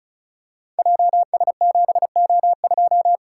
Morse code, heard as a keyed tone, about 36 wpm, 700 Hz. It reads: JS7O2
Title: Code Groups - 5 Letter:Numbers - 35wpm